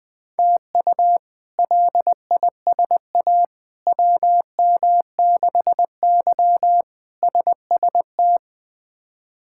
Morse code, heard as a keyed tone, about 20 wpm, 700 Hz. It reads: TU LISA WM6Y SST
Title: K1USN SST QSO Practice - 20wpm